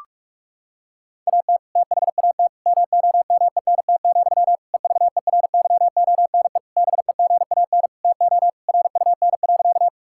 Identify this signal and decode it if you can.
Morse code, heard as a keyed tone, about 46 wpm, 700 Hz. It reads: AT THAT MOMENT, EVERYONE BEGAN TO RUN.
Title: Sentences from Top 1000 Words - Review Part 5 - 45wpm